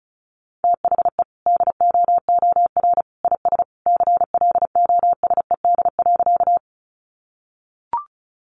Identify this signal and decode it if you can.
Morse code, heard as a keyed tone, about 34 wpm, 700 Hz. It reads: THE DOOR IS CLOSED.
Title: Sentences from Top 1400 Words 35wpm